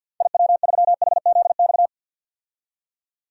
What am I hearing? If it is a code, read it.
Morse code, heard as a keyed tone, about 52 wpm, 700 Hz. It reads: IW3HZX
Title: DXCC-100 Call Signs - 6 Characters - 50wpm